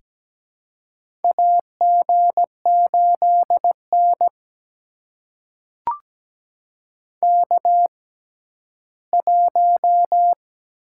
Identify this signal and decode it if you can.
Morse code, heard as a keyed tone, about 17 wpm, spaced as for 11 wpm, 700 Hz. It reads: AG8N K1
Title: USA Call Signs - 4 Characters - 6x Character Spacing - 17wpm